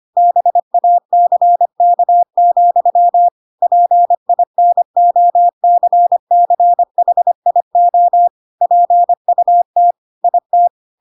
Morse code, heard as a keyed tone, about 25 wpm, 700 Hz. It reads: BACK, PINOCCHIO PUT IT
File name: Pinocchio / Pinocchio ch3 25 25 ik1hge